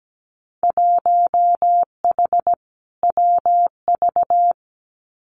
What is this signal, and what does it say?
Morse code, heard as a keyed tone, about 17 wpm, 700 Hz. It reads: 1H WV
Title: Winter Field Day Exchanges 17wpm